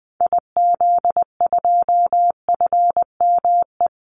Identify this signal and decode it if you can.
Morse code, heard as a keyed tone, about 20 wpm, 700 Hz. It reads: IZ2FME
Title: Call Signs - 20wpm